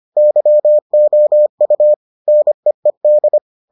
Morse code, heard as a keyed tone, about 25 wpm, 600 Hz. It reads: YOU NEED